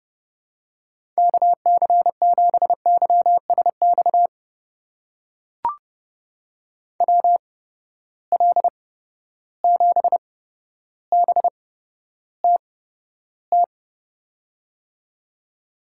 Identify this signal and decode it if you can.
Morse code, heard as a keyed tone, about 32 wpm, spaced as for 13 wpm, 700 Hz. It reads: KC7YSX WL7BTT
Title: USA Call Signs - 6 Characters - 8x Character Spacing - 30wpm